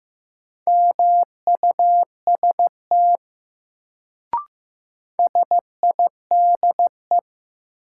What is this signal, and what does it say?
Morse code, heard as a keyed tone, about 15 wpm, 700 Hz. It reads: MUST SIDE
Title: Sets of 2 Words - New Word Drill - 15wpm